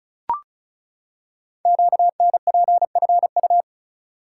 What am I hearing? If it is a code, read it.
Morse code, heard as a keyed tone, about 33 wpm, 700 Hz. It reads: QNPFU
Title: Code Groups - 5 Letters - 35wpm